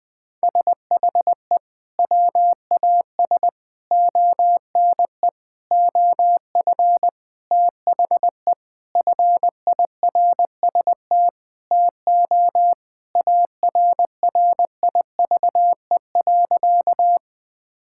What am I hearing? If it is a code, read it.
Morse code, heard as a keyed tone, about 20 wpm, 700 Hz. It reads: SHE WAS ONE OF THE FIRST TO ARRIVE.
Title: Sentences with 6-Word Phrases - Spoken Beginning - 20wpm